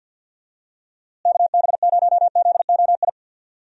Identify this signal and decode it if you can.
Morse code, heard as a keyed tone, about 46 wpm, 700 Hz. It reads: KB0ZOI
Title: USA Call Signs - 6 Characters - Repeated 2x - 50wpm